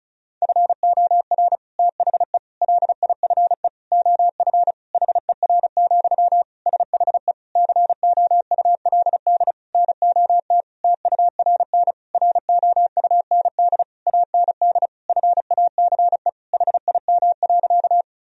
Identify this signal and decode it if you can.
Morse code, heard as a keyed tone, about 36 wpm, 700 Hz. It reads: FOR THE LIFE OF HER, SHE COULD NOT TURN ROUND AND FACE HIM.
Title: Sentences from Top 1000 Words - Review Part 2 - 35wpm